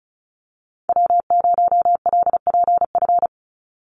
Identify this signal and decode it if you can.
Morse code, heard as a keyed tone, about 34 wpm, 700 Hz. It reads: W0LPF
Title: Call Signs Repeated 3x - 35wpm